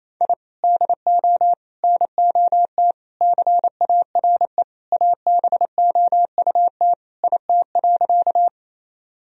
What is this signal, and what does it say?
Morse code, heard as a keyed tone, about 28 wpm, 700 Hz. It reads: I DO NOT CARE ABOUT IT.
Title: Sentences from Top 1000 Words - Review Part 1 - 28wpm